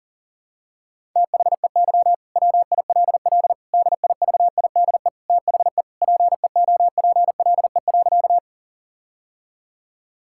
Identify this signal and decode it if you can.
Morse code, heard as a keyed tone, about 40 wpm, 700 Hz. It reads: THEY WILL DIVIDE THE PEOPLE.
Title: Sentences from Top 1600 Words 40wpm